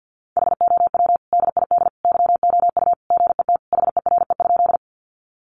Morse code, heard as a keyed tone, about 50 wpm, 700 Hz. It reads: HOW DID YOU GET HERE?
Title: Sentences from Top 200 Words 50wpm